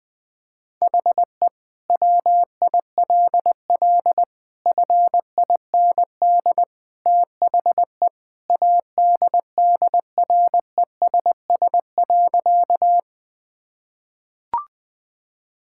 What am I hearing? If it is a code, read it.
Morse code, heard as a keyed tone, about 21 wpm, 700 Hz. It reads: HE WILL FIND THE ADDRESS.
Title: Sentences from Top 1000 Words - Review Part 6 - 20wpm